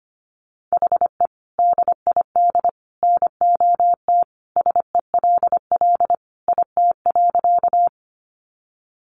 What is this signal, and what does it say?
Morse code, heard as a keyed tone, about 25 wpm, 700 Hz. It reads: HE DID NOT SELL IT.
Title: Sentences from Top 1100 Words 25wpm